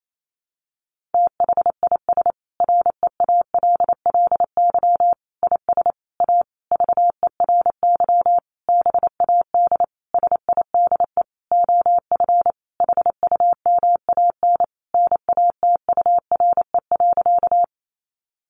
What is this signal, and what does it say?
Morse code, heard as a keyed tone, about 28 wpm, 700 Hz. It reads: THIS REALLY IS A VERY BAD SIDE OF HUMAN NATURE.
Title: Sentences from Top 700 Words 28wpm